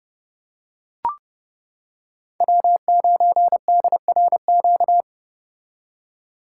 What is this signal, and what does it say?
Morse code, heard as a keyed tone, about 31 wpm, 700 Hz. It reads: W9DRQ